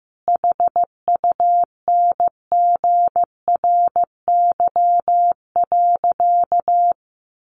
Morse code, heard as a keyed tone, about 15 wpm, 700 Hz. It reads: HUNGRY.
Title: Sentences from Top 1400 Words 15wpm